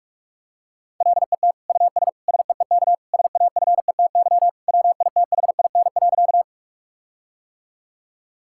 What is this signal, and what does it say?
Morse code, heard as a keyed tone, about 46 wpm, 700 Hz. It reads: LET US SEEK SAFETY WITHIN.
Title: Sentences from Top 1600 Words 45wpm